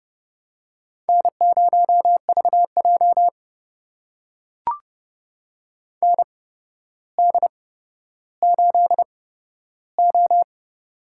Morse code, heard as a keyed tone, about 29 wpm, spaced as for 13 wpm, 700 Hz. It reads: N0VJ ND8O
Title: USA Call Signs - 4 Characters - 8x Character Spacing - 30wpm